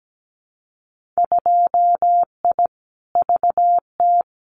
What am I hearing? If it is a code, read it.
Morse code, heard as a keyed tone, about 17 wpm, 700 Hz. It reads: 2I VT
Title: Winter Field Day Exchanges 17wpm